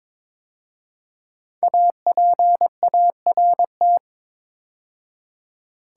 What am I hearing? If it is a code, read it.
Morse code, heard as a keyed tone, about 22 wpm, 700 Hz. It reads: APART
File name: 5 Letter Words - First 3 Letters Given - 22wpm